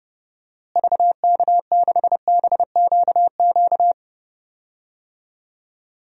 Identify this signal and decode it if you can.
Morse code, heard as a keyed tone, about 30 wpm, 700 Hz. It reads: VK6BQQ
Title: DXCC-100 Call Signs - 6 Characters - Repeated 2x - 30wpm